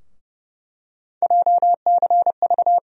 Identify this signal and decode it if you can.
Morse code, heard as a keyed tone, about 31 wpm, 700 Hz. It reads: JCV
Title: USA Call Sign Postfixes - 3 Letters 30wpm